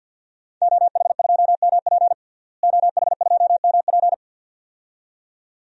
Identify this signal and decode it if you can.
Morse code, heard as a keyed tone, about 49 wpm, 700 Hz. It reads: OH2MP OH2MP
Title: DXCC-100 Call Signs - 5 Characters - Repeated 2x - 50wpm